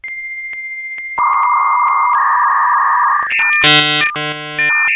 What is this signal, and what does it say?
Dial-up modem handshake.